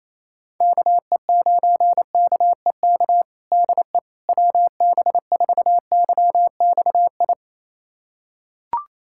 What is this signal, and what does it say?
Morse code, heard as a keyed tone, about 28 wpm, 700 Hz. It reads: KE9KEK DE WB4YXI
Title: USA Callsign DE Callsign - 6 Characters - 28wpm